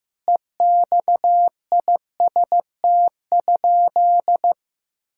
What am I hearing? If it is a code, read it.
Morse code, heard as a keyed tone, about 15 wpm, 700 Hz. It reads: EXIST?
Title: Sentences from Top 1000 Words - Review Part 5 - 15wpm